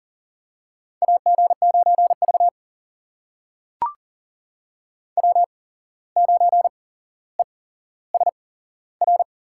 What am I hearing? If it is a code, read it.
Morse code, heard as a keyed tone, about 40 wpm, spaced as for 16 wpm, 700 Hz. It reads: AG9V W9ESR
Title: USA Call Signs - 8x Character Spacing - 40wpm